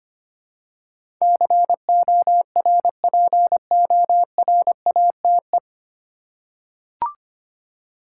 Morse code, heard as a keyed tone, about 25 wpm, 700 Hz. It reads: CORPORATE
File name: Nine Letter Words - 25wpm